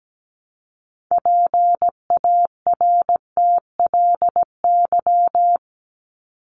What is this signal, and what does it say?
Morse code, heard as a keyed tone, about 17 wpm, 700 Hz. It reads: PARTLY
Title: LY Suffix - Top 100 Words with Base Words 17wpm